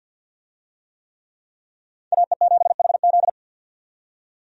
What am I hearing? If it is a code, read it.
Morse code, heard as a keyed tone, about 47 wpm, 700 Hz. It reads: AE7HZ
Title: USA Call Signs - 5 Characters - 50wpm